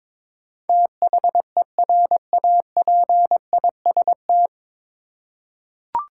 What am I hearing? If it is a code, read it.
Morse code, heard as a keyed tone, about 22 wpm, 700 Hz. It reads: THERAPIST
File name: 9 Letter Words - First 3 Letters Given - 22wpm